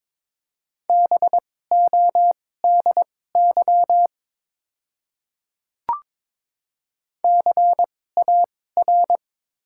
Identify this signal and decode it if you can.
Morse code, heard as a keyed tone, about 22 wpm, spaced as for 16 wpm, 690 Hz. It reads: BODY CAR
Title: 2x Character Spacing - Top 500 Words - 22wpm